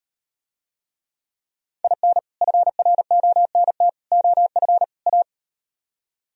Morse code, heard as a keyed tone, about 38 wpm, 700 Hz. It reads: IN FRONT OF A